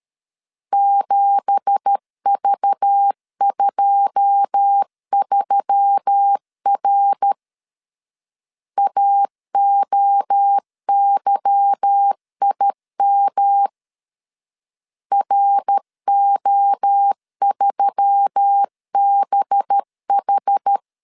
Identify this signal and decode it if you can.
Morse code, heard as a keyed tone, about 13 wpm, 800 Hz. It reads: 7V23R AOYIM RO3BH
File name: LEZIONI / 9 2867 12wpm 2322